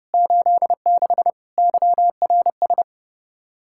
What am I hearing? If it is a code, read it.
Morse code, heard as a keyed tone, about 30 wpm, 700 Hz. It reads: 86 YRS
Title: K1USN SST QSO Practice 30wpm